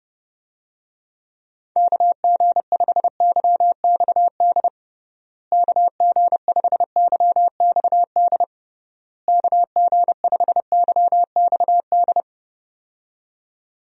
Morse code, heard as a keyed tone, about 30 wpm, 700 Hz. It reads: KG5YXD KG5YXD KG5YXD
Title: USA Call Signs - 6 Characters - Repeated 3x - 30wpm